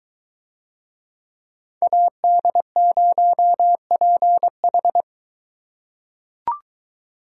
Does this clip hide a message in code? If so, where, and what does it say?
Yes: Morse code from 1.8 to 5.0 s, heard as a keyed tone, about 22 wpm, 700 Hz. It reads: AD0PH